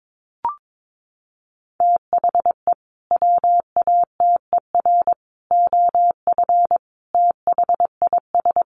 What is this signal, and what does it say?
Morse code, heard as a keyed tone, about 22 wpm, 700 Hz. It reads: THE WATER OF THIS
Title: Sentences from Top 1500 Words 22wpm